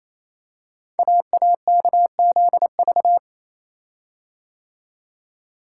Morse code, heard as a keyed tone, about 27 wpm, 700 Hz. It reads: AAKZV
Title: Code Groups - 5 Letter:Numbers - 28wpm